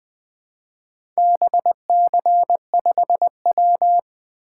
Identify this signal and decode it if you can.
Morse code, heard as a keyed tone, about 20 wpm, 700 Hz. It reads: BC5W
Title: Four Letter-Number Combinations - 20wpm